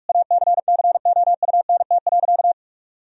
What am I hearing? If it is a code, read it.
Morse code, heard as a keyed tone, about 42 wpm, 700 Hz. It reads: ACCOUNT.
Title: Sentences from Top 1000 Words - Review Part 4 - 45wpm